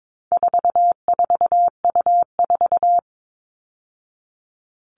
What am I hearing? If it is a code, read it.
Morse code, heard as a keyed tone, about 22 wpm, 700 Hz. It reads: 44U4
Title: All Cut Numbers - 4-Digit 22wpm